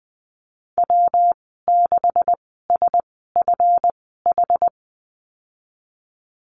Morse code, heard as a keyed tone, about 20 wpm, spaced as for 15 wpm, 700 Hz. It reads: W6SFH
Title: 147 Lesson 24g - Letter y - Call Signs 20wpm